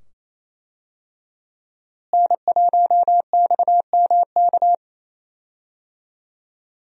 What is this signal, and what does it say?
Morse code, heard as a keyed tone, about 28 wpm, 700 Hz. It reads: N1XMK
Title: USA Call Signs - 5 Characters - Repeated 2x - 28wpm